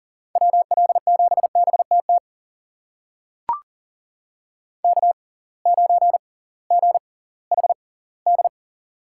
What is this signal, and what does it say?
Morse code, heard as a keyed tone, about 40 wpm, spaced as for 22 wpm, 700 Hz. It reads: WL7BTT K9GHD
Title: USA Call Signs - 6x Character Spacing - 40wpm